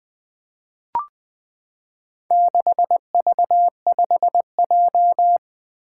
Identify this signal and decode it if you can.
Morse code, heard as a keyed tone, about 21 wpm, 700 Hz. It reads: 6V5J
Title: Four Challenging Letter-Number Combinations 20wpm